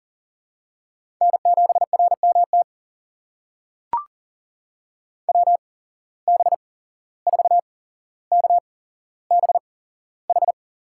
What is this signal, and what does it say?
Morse code, heard as a keyed tone, about 39 wpm, spaced as for 16 wpm, 700 Hz. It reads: N7RMT WB4KBH
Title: USA Call Signs - 8x Character Spacing - 40wpm